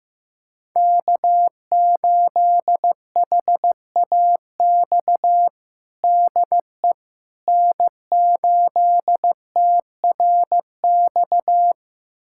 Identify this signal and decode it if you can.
Morse code, heard as a keyed tone, about 15 wpm, 700 Hz. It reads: K8HAX DE N8TRX
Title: USA Callsign DE Callsign - 5 Characters - 15wpm